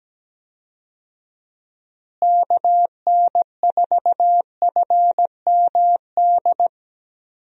Morse code, heard as a keyed tone, about 17 wpm, 700 Hz. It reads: KN4FMD
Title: USA Call Signs - 6 Characters - Repeated 3x - 17wpm